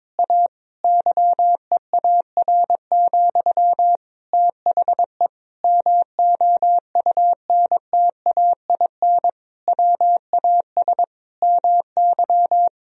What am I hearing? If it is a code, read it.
Morse code, heard as a keyed tone, about 22 wpm, 700 Hz. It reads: A YEAR, THE MOUNTAIN WAS MY
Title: Sentences from Top 1000 Words - Review Part 1 - 22wpm